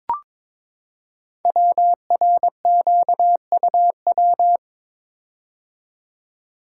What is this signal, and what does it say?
Morse code, heard as a keyed tone, about 22 wpm, 700 Hz. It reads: WRQUW
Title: Code Groups - 5 Letters - 22wpm